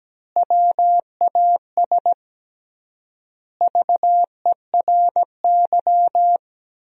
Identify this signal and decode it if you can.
Morse code, heard as a keyed tone, about 17 wpm, 700 Hz. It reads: WAS VERY
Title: 3x Word-Spacing - Sentences from Top 300 Words - 17wpm